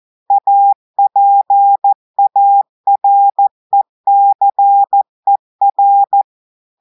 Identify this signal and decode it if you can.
Morse code, heard as a keyed tone, about 14 wpm, 800 Hz. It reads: APARECER
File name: audios / portugais / portugais texte en clair 1 vitesse 14 mots minutes